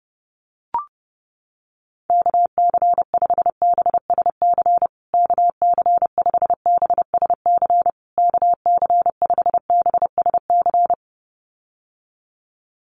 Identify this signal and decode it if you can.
Morse code, heard as a keyed tone, about 28 wpm, 700 Hz. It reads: KC5BSC KC5BSC KC5BSC
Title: USA Call Signs Repeated 3x 30wpm